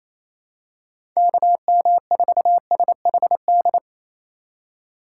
Morse code, heard as a keyed tone, about 28 wpm, 700 Hz. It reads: KM4SHD
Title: USA Call Signs - 6 Characters - 4x Character Spacing - 28wpm